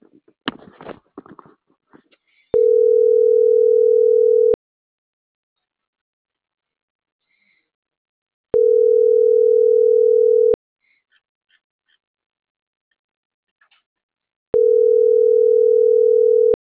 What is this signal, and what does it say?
Ringback tone (North America), 440 + 480 Hz.